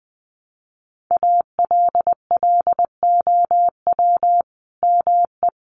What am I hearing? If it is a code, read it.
Morse code, heard as a keyed tone, about 20 wpm, 700 Hz. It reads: ALLOW ME
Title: Sentences from Top 1100 Words 20wpm